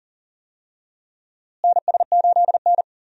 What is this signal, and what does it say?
Morse code, heard as a keyed tone, about 40 wpm, 700 Hz. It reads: NS8N
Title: USA Call Signs - 4 Characters - Repeated 3x - 40wpm